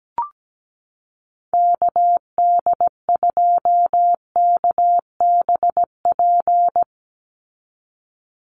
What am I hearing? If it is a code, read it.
Morse code, heard as a keyed tone, about 17 wpm, 700 Hz. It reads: KD2KBP
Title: USA Call Signs 17wpm